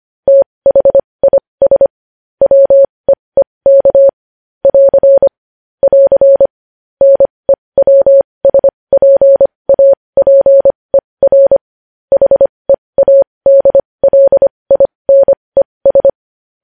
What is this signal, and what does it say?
Morse code, heard as a keyed tone, about 25 wpm, spaced as for 20 wpm, 550 Hz. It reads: THIS WEEK <AR> <AR> NEWSPAPER HEADLINES
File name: podcast / News headlines-1x-25x20